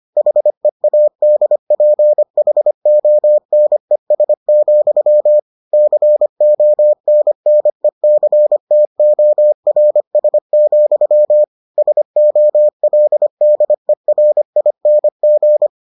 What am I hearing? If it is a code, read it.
Morse code, heard as a keyed tone, about 25 wpm, 600 Hz. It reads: HEADPHONES, CONNECTORS, SOLDERING